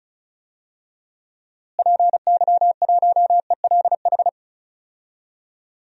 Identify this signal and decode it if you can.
Morse code, heard as a keyed tone, about 34 wpm, 700 Hz. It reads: PY1ELH
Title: DXCC-100 Call Signs - 6 Characters - Repeated 3x - 35wpm